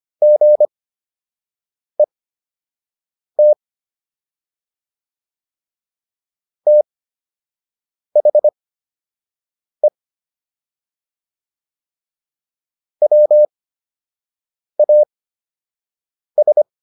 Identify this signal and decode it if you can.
Morse code, heard as a keyed tone, about 25 wpm, spaced as for 6 wpm, 600 Hz. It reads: GET THE WAS